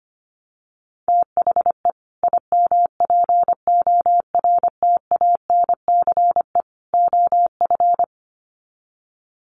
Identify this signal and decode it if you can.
Morse code, heard as a keyed tone, about 25 wpm, 700 Hz. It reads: THE IMPORTANCE OF
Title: Sets of 3 Words 25wpm